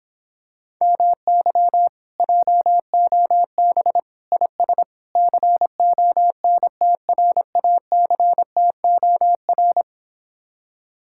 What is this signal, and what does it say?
Morse code, heard as a keyed tone, about 26 wpm, 700 Hz. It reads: MY JOB IS CONTRACTOR